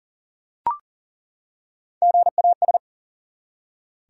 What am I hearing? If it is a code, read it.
Morse code, heard as a keyed tone, about 40 wpm, 700 Hz. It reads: GAS
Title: Three Letter Words - 40wpm